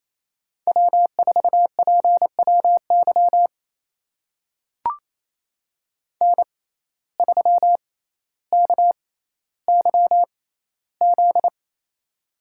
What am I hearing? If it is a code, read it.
Morse code, heard as a keyed tone, about 28 wpm, spaced as for 14 wpm, 700 Hz. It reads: W4PWY N3KYZ
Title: USA Call Signs - 5 Characters - 6x Character Spacing - 28wpm